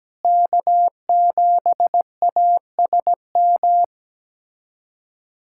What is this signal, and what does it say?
Morse code, heard as a keyed tone, about 17 wpm, 700 Hz. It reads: K7ASM
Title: USA Callsign DE Callsign - 5 Characters - 17wpm